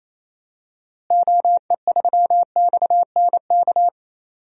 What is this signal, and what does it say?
Morse code, heard as a keyed tone, about 28 wpm, 700 Hz. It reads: OE3XNK
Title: DXCC-100 Call Signs - 6 Characters - Repeated 3x - 28wpm